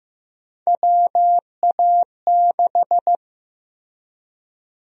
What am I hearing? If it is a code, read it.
Morse code, heard as a keyed tone, about 15 wpm, 700 Hz. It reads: WA6
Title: USA Call Sign Prefixes - Repeated 2x - 15wpm